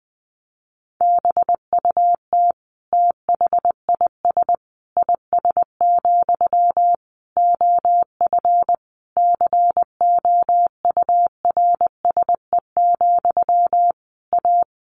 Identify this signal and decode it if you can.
Morse code, heard as a keyed tone, about 20 wpm, 700 Hz. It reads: BUT THIS IS, OF COURSE, A
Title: Sentences from Top 1500 Words 20wpm